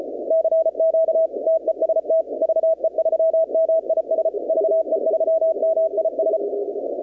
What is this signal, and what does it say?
Morse code, heard as a keyed tone, about 33 wpm, 660 Hz. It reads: CQ TEST VE3MIS VE3MIS
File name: contests / 2019-cw-wpx-kt5j-k5pi- -2330 0-21